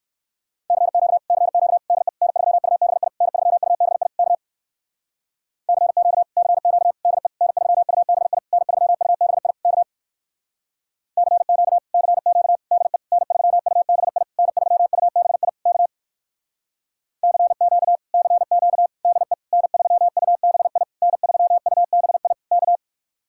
Morse code, heard as a keyed tone, about 59 wpm, 700 Hz. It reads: CQ CQ DE N3UBI N3UBI K CQ CQ DE N3UBI N3UBI K CQ CQ DE N3UBI N3UBI K CQ CQ DE N3UBI N3UBI K